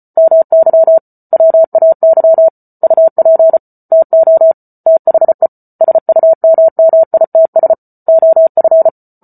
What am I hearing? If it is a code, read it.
Morse code, heard as a keyed tone, about 34 wpm, 650 Hz. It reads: MY WAY UP TO THE SUMMITS OF